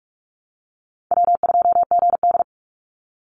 Morse code, heard as a keyed tone, about 44 wpm, 700 Hz. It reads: W2GD
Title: Easy Call Signs - 45wpm